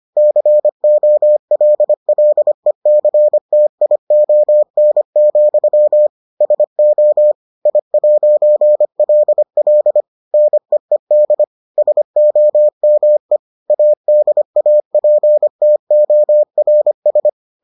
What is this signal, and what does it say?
Morse code, heard as a keyed tone, about 25 wpm, 600 Hz. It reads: COLLECTION, SO I'LL NEED SOME ADAPTORS